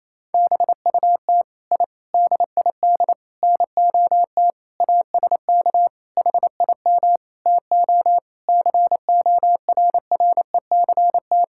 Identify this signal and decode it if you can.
Morse code, heard as a keyed tone, about 28 wpm, 700 Hz. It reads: BUT I DID NOT ASK HIM TO CORRECT
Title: Sentences from Top 1000 Words - Review Part 5 - 28wpm